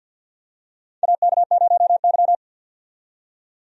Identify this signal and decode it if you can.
Morse code, heard as a keyed tone, about 49 wpm, 700 Hz. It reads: AK0Y
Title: USA Call Signs - 4 Characters - 50wpm